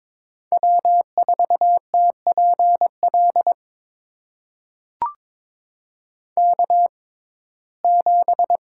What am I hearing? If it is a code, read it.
Morse code, heard as a keyed tone, about 22 wpm, spaced as for 16 wpm, 700 Hz. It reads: W4TPL K7
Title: USA Call Signs - 5 Characters - 6x Character Spacing - 22wpm